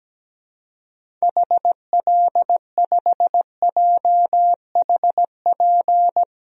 Morse code, heard as a keyed tone, about 17 wpm, 700 Hz. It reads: HL5JHP
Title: DXCC-100 Call Signs - 6 Characters - Repeated 2x - 17wpm